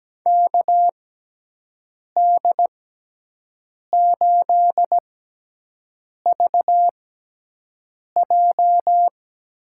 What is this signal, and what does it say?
Morse code, heard as a keyed tone, about 17 wpm, spaced as for 6 wpm, 700 Hz. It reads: KD8VJ
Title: USA Call Signs - 6 Characters - 6x Character Spacing - 17wpm